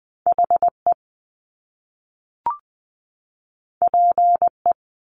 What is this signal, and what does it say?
Morse code, heard as a keyed tone, about 20 wpm, 700 Hz. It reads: HE PE
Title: Top Two Letter Combinations in Words 20wpm